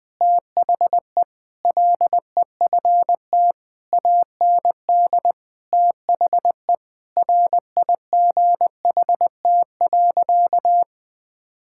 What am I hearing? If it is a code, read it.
Morse code, heard as a keyed tone, about 20 wpm, 700 Hz. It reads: THE LEFT AND THE RIGHT.
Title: Sentences with 7-Word Phrases - 20wpm